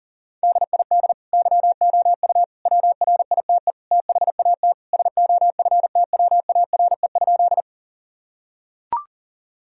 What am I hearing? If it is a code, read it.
Morse code, heard as a keyed tone, about 40 wpm, 700 Hz. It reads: DID YOU WRITE THAT SOFTWARE?
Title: Sentences from Top 1000 Words - Review Part 3 - 40wpm